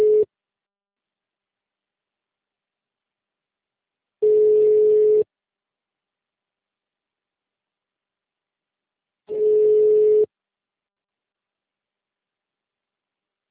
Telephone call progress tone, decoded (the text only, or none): ringback tone